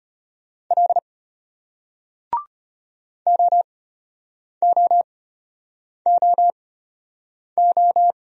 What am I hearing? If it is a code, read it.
Morse code, heard as a keyed tone, about 34 wpm, spaced as for 8 wpm, 700 Hz. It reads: L OOOO